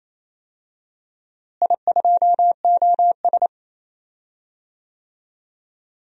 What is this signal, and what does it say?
Morse code, heard as a keyed tone, about 28 wpm, 700 Hz. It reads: I2OS
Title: DXCC-100 Call Signs - 4 Characters - Repeated 3x - 28wpm